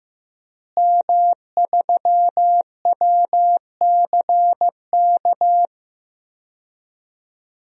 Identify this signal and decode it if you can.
Morse code, heard as a keyed tone, about 16 wpm, 700 Hz. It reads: M3WCK
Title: DXCC-100 Call Signs - 5 Characters - Repeated 2x - 15wpm